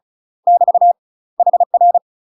Morse code, heard as a keyed tone, about 35 wpm, 700 Hz. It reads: <BT> HR
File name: UK MASTER QSOs TRAINING / 35WPM / UK MASTER QSOs 0023